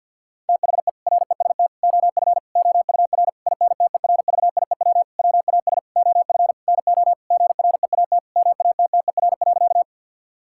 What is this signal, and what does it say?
Morse code, heard as a keyed tone, about 50 wpm, 700 Hz. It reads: THE REST OF OUR INTERVIEW WAS OF NO GREAT MATTER.
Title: Sentences from Top 1600 Words 50wpm